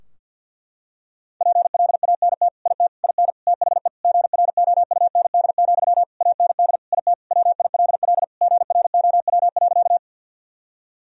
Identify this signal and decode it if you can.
Morse code, heard as a keyed tone, about 52 wpm, 700 Hz. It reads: PLANT IT IN THE GROUND, AND IT WILL GROW.
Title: Sentences from Top 1000 Words - Review Part 2 - 50wpm